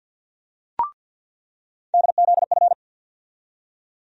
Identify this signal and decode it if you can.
Morse code, heard as a keyed tone, about 47 wpm, 700 Hz. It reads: DZF
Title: USA Call Sign Postfixes - 3 Letters 50wpm